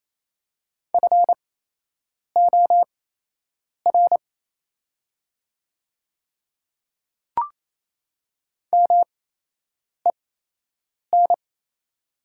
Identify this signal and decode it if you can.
Morse code, heard as a keyed tone, about 28 wpm, spaced as for 8 wpm, 700 Hz. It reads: FOR MEN
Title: 8x Character Spacing - Top 200 Words - 28wpm